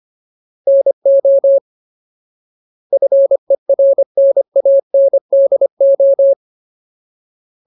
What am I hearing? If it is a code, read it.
Morse code, heard as a keyed tone, about 25 wpm, 550 Hz. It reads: NO FERNANDO
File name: Ultimate Word List / 25WPM Ultimate